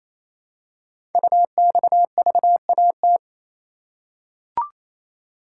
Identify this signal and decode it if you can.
Morse code, heard as a keyed tone, about 28 wpm, 700 Hz. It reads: UXVAT
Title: Code Groups - 5 Letters - 28wpm